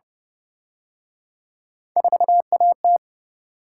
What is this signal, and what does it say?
Morse code, heard as a keyed tone, about 30 wpm, 700 Hz. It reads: 4AT